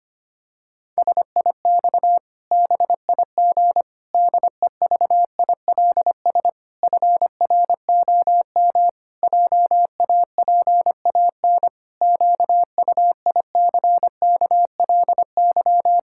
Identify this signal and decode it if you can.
Morse code, heard as a keyed tone, about 25 wpm, 700 Hz. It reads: SIX BIG DEVILS FROM JAPAN QUICKLY
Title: Pangram Challenge v2 - 25wpm